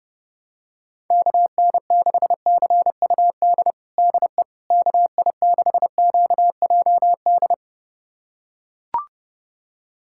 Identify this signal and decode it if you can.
Morse code, heard as a keyed tone, about 30 wpm, 700 Hz. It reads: KN6CUD DE KI6QJD
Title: USA Callsign DE Callsign - 6 Characters - 30wpm